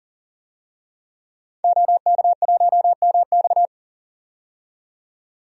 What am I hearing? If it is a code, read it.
Morse code, heard as a keyed tone, about 40 wpm, 700 Hz. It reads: OK1MX
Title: DXCC-100 Call Signs - 5 Characters - 40wpm